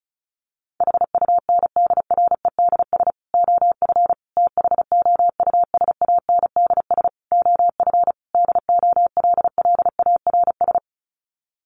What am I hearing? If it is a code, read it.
Morse code, heard as a keyed tone, about 34 wpm, 700 Hz. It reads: HUNDREDS OF THOUSANDS OF DOLLARS
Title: Sets of 5 Words Encore 35wpm